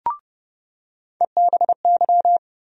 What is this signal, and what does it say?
Morse code, heard as a keyed tone, about 30 wpm, 700 Hz. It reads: EBY
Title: USA Call Sign Postfixes - 3 Letters 30wpm